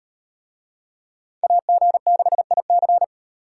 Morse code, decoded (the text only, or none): AG6IC